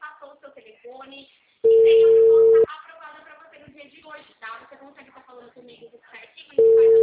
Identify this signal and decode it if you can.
Ringback tone (China, 450 Hz), with speech.